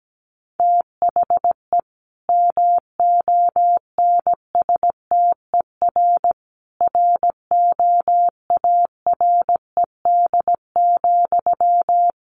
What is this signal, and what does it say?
Morse code, heard as a keyed tone, about 17 wpm, 700 Hz. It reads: THE MONSTER ROARED,
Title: A Sound Of Thunder 17wpm